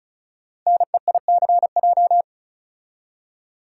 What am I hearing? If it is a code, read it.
Morse code, heard as a keyed tone, about 36 wpm, 700 Hz. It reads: NEICJ
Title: Code Groups - 5 Letters - 35wpm